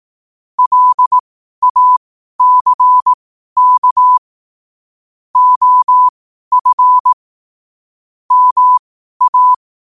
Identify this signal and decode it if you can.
Morse code, heard as a keyed tone, about 18 wpm, spaced as for 12 wpm, 1000 Hz. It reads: LACK OF MA
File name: qotdfiles / 12wpm